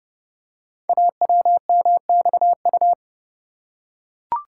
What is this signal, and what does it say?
Morse code, heard as a keyed tone, about 29 wpm, 700 Hz. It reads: AWMXU
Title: Code Groups - 5 Letters - 30wpm